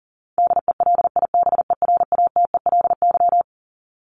Morse code, heard as a keyed tone, about 40 wpm, 700 Hz. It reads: DELIBERATELY
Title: Top 1751-2000 Words 40wpm